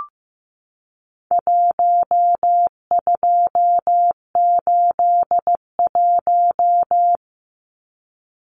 Morse code, heard as a keyed tone, about 15 wpm, 700 Hz. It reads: 1281
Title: Numbers - 4-Digit 15wpm